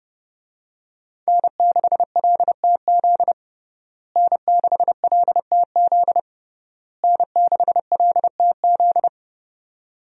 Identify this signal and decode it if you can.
Morse code, heard as a keyed tone, about 30 wpm, 700 Hz. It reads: N6LTZ N6LTZ N6LTZ
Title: USA Call Signs - 5 Characters - Repeated 3x - 30wpm